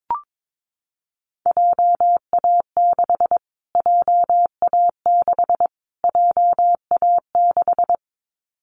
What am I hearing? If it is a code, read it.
Morse code, heard as a keyed tone, about 22 wpm, 700 Hz. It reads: JA6 JA6 JA6
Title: Callsign Prefixes for Top 25 DXCC - Repeated 3x - 22wpm